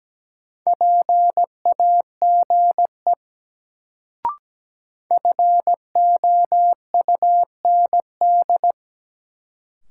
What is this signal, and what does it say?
Morse code, heard as a keyed tone, about 17 wpm, 700 Hz. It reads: PAGE FOUND
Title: Sets of 5 Words Encore - New Word Drill - 17wpm